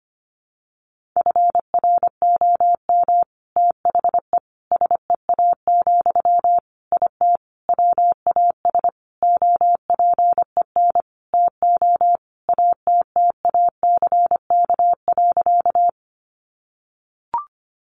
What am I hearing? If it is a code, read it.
Morse code, heard as a keyed tone, about 25 wpm, 700 Hz. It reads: FROM THE SEA, IT WAS OPEN TO ATTACK.
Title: Sentences from Top 1500 Words 25wpm